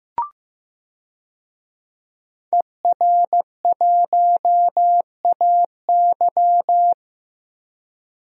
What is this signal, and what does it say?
Morse code, heard as a keyed tone, about 15 wpm, 700 Hz. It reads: ER1AY
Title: DXCC-100 Call Signs - 5 Characters - Repeated 2x - 15wpm